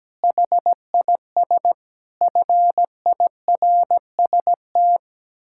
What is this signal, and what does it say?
Morse code, heard as a keyed tone, about 17 wpm, 700 Hz. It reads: HIS FIRST
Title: Pangram Challenge v2 - 17wpm